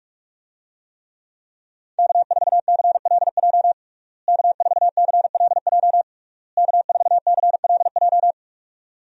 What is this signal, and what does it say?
Morse code, heard as a keyed tone, about 46 wpm, 700 Hz. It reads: K4CLJ K4CLJ K4CLJ
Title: USA Call Signs - 5 Characters - Repeated 3x - 45wpm